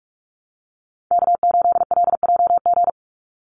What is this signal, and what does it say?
Morse code, heard as a keyed tone, about 44 wpm, 700 Hz. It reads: K8LJG
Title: NA SOTA Chaser Callsigns 45wpm